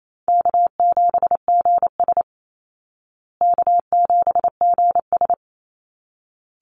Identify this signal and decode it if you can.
Morse code, heard as a keyed tone, about 28 wpm, 700 Hz. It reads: K7GS K7GS
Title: Easy Call Signs Repeated 2x - 28wpm